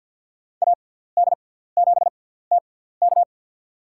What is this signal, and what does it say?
Morse code, heard as a keyed tone, about 48 wpm, spaced as for 17 wpm, 700 Hz. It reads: AD7TK
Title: USA Call Signs - 5 Characters - 6x Character Spacing - 50wpm